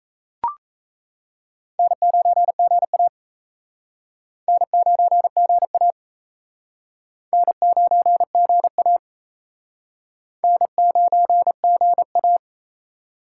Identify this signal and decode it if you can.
Morse code, heard as a keyed tone, about 36 wpm, 700 Hz. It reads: N9GA N9GA N9GA N9GA